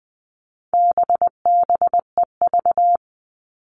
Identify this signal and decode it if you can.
Morse code, heard as a keyed tone, about 20 wpm, 700 Hz. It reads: BBEV
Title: All Cut Numbers - 4-Digit 20wpm